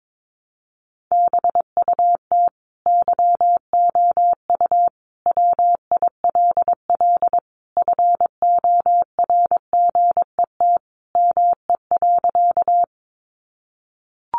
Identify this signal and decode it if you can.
Morse code, heard as a keyed tone, about 22 wpm, 700 Hz. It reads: BUT YOU WILL FORGET ME.
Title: Sentences from Top 1200 Words 22wpm